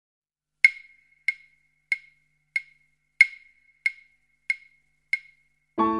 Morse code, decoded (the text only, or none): none